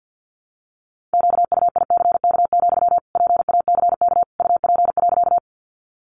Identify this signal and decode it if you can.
Morse code, heard as a keyed tone, about 50 wpm, 700 Hz. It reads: QUICK, PACK UP.
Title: Sentences from Top 1300 Words 50wpm